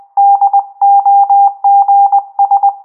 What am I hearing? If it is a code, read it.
Morse code, heard as a keyed tone, about 20 wpm, 800 Hz. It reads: DOGS
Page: Trasmissione Morse realizzata automaticamente